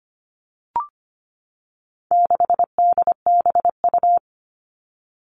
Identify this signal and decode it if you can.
Morse code, heard as a keyed tone, about 25 wpm, 700 Hz. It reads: 6DBU
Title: All Cut Numbers - 4-Digit 25wpm